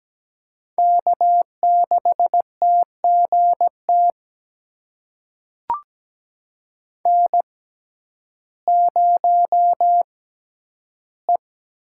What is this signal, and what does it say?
Morse code, heard as a keyed tone, about 17 wpm, spaced as for 10 wpm, 700 Hz. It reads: K6TGT N0E
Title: USA Call Signs - 5 Characters - 6x Character Spacing - 17wpm